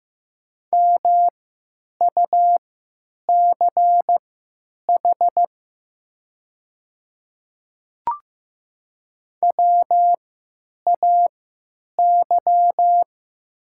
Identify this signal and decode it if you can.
Morse code, heard as a keyed tone, about 15 wpm, spaced as for 9 wpm, 700 Hz. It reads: MUCH WAY
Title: 3x Character Spacing - Top 300 Words - 15wpm